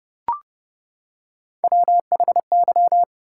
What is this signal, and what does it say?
Morse code, heard as a keyed tone, about 32 wpm, 700 Hz. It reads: WHY
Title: Three Letter Words - 30wpm